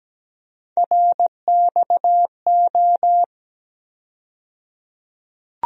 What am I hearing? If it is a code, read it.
Morse code, heard as a keyed tone, about 17 wpm, 700 Hz. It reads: RXO
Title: USA Call Sign Postfixes - 3 Letters 17wpm